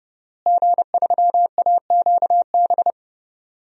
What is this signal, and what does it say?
Morse code, heard as a keyed tone, about 32 wpm, 700 Hz. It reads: G3AQB
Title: DXCC-100 Call Signs - 5 Characters - Repeated 2x - 30wpm